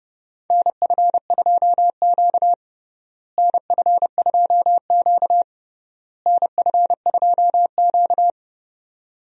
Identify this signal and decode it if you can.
Morse code, heard as a keyed tone, about 30 wpm, 700 Hz. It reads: NF2Q NF2Q NF2Q
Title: USA Call Signs - 4 Characters - Repeated 3x - 30wpm